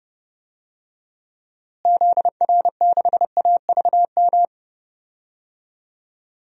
Morse code, heard as a keyed tone, about 32 wpm, 700 Hz. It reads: ZR6AVM
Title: DXCC-100 Call Signs - 6 Characters - Repeated 3x - 30wpm